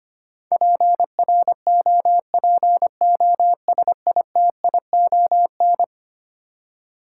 Morse code, heard as a keyed tone, about 25 wpm, 700 Hz. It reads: PROPOSITION
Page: morse-code-ninja-practice-25wpm